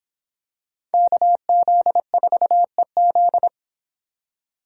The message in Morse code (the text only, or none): KZ4EZ